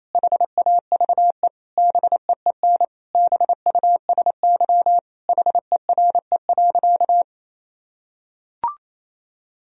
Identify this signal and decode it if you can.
Morse code, heard as a keyed tone, about 28 wpm, 700 Hz. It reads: HAVE BEEN BUSY HERE.
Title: Sentences from Top 1000 Words - Review Part 3 - 28wpm